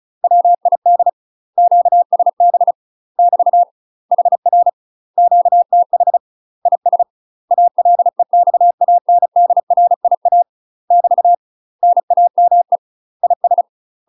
Morse code, heard as a keyed tone, about 34 wpm, 700 Hz. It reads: WID QSB <BT> HR QTH IS ALEXANDRIA <BT> NAME IS
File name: UK MASTER QSOs TRAINING / 35WPM / UK MASTER QSOs 0043